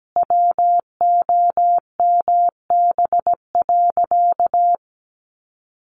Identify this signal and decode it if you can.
Morse code, heard as a keyed tone, about 17 wpm, 700 Hz. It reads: WOMB.
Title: A Sound Of Thunder 17wpm